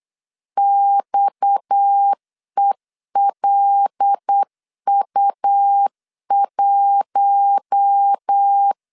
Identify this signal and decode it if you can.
Morse code, heard as a keyed tone, about 8 wpm, 800 Hz. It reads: XELU1
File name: LEZIONI / 9 2 8wpm 3437